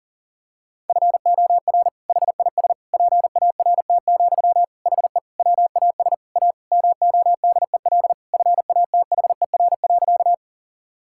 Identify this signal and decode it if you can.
Morse code, heard as a keyed tone, about 37 wpm, 700 Hz. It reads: FOR HIS PART, HE WAS A MODEL FATHER.
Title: Sentences from Top 1000 Words - Review Part 5 - 40wpm